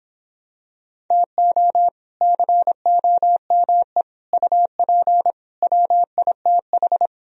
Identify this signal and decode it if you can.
Morse code, heard as a keyed tone, about 26 wpm, 700 Hz. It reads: TO COME UP WITH